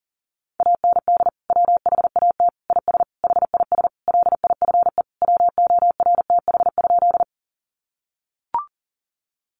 Morse code, heard as a keyed tone, about 40 wpm, 700 Hz. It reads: AND WHAT IS HIS LIFE WORTH?
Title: Sentences from Top 800 Words 40wpm